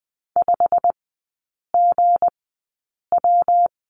Morse code, heard as a keyed tone, about 20 wpm, spaced as for 8 wpm, 700 Hz. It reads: 5GW
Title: 159 Lesson 26e - Letter g - Three Characters 20wpm